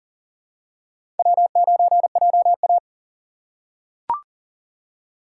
Morse code, heard as a keyed tone, about 37 wpm, 690 Hz. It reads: W9JA
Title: USA Call Signs - 4 Characters - 2x Character Spacing - 40wpm